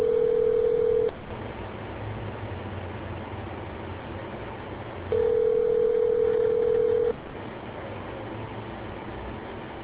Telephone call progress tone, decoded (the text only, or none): ringback tone